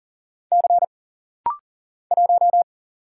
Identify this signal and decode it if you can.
Morse code, heard as a keyed tone, about 40 wpm, 700 Hz. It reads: C 1
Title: Single Letter-Number Alt - Rapid-Fire 40wpm